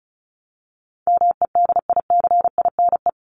Morse code, heard as a keyed tone, about 34 wpm, 700 Hz. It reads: MEDICINE
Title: Top 751-1000 Words 35wpm